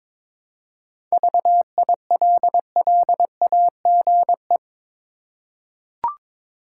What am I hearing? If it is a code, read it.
Morse code, heard as a keyed tone, about 22 wpm, 700 Hz. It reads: VILLAGE
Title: Seven Letter Words - 22wpm